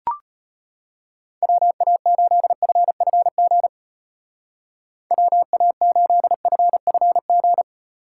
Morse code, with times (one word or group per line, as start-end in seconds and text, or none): 1.4-3.7 WA8FFG
5.1-7.6 WA8FFG